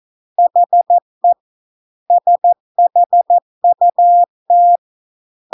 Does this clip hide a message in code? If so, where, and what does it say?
Yes: Morse code from 0.4 to 4.8 s, heard as a keyed tone, about 14 wpm, 700 Hz. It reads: HE SHUT